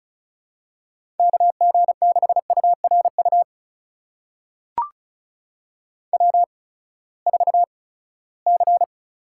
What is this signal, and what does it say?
Morse code, heard as a keyed tone, about 36 wpm, spaced as for 20 wpm, 700 Hz. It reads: KG6URU W4C
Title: USA Call Signs - 8x Character Spacing - 35wpm